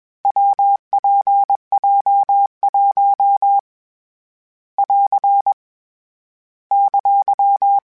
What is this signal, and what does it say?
Morse code, heard as a keyed tone, about 21 wpm, 800 Hz. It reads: WPJ1 <AR> !
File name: Difficult Combos / Difficult Combos-mp3s / WPJ WPJ1 @20